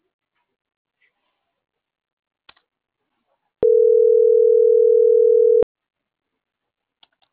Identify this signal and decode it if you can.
Ringback tone (North America), 440 + 480 Hz.